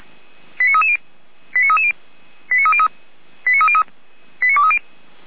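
Five-tone selective call (ZVEI), heard as quick runs of five tones, tones of about 80 ms. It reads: 89330 89330 89303 89303 89230
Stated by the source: ZVEI five-tone call 89319 received by SDR, from feuerwehr-orlamuende.org (Orlamünde fire brigade) audio archive.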